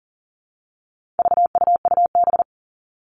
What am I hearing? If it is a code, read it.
Morse code, heard as a keyed tone, about 37 wpm, 700 Hz. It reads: VUUB
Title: All Cut Numbers - 4-Digit 40wpm